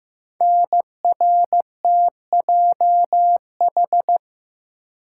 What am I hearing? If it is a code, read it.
Morse code, heard as a keyed tone, about 15 wpm, 700 Hz. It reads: NRTJH
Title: Code Groups - 5 Letters - 15wpm